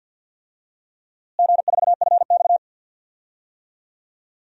Morse code, heard as a keyed tone, about 52 wpm, 700 Hz. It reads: G4FX
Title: DXCC-100 Call Signs - 4 Characters - Repeated 2x - 50wpm